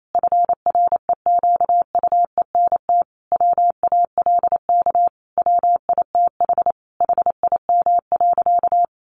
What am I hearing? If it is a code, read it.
Morse code, heard as a keyed tone, about 28 wpm, 700 Hz. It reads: FREQUENT WALK WITH HIM.
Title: Sentences from Top 1400 Words 28wpm